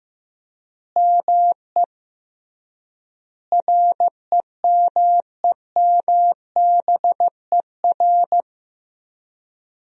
Morse code, heard as a keyed tone, about 15 wpm, 700 Hz. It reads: ME REMEMBER
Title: 3x Word-Spacing - Sentences from Top 400 Words - 15wpm